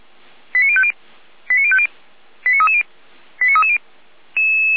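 Five-tone selective call (ZVEI), heard as quick runs of five tones, tones of about 70 ms. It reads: 89050 89050 89330 89330
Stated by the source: ZVEI five-tone call received by SDR, from feuerwehr-orlamuende fire service audio archive.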